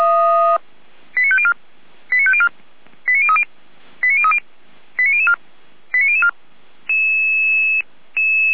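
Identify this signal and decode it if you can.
Five-tone selective call (ZVEI), heard as quick runs of five tones, tones of about 80 ms. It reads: 89504 89504 89030 89030 89004 89004
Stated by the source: ZVEI five-tone call 89099 received on an SDR, from the Feuerwehr Orlamünde fire brigade server.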